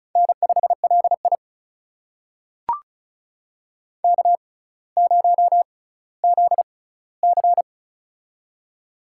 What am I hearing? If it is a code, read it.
Morse code, heard as a keyed tone, about 36 wpm, spaced as for 18 wpm, 700 Hz. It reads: N5LI K0ZC
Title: USA Call Signs - 6x Character Spacing - 35wpm